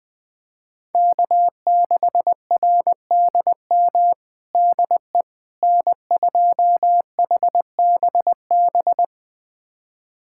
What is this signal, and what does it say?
Morse code, heard as a keyed tone, about 20 wpm, 700 Hz. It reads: K6RDM DE N2HBB
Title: USA Callsign DE Callsign - 5 Characters - 20wpm